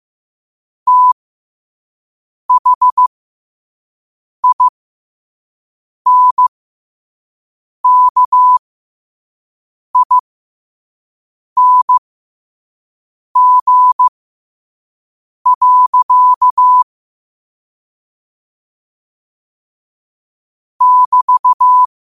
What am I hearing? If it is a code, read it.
Morse code, heard as a keyed tone, about 15 wpm, spaced as for 5 wpm, 1000 Hz. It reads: THINKING. <BT>
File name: qotdfiles / 5wpm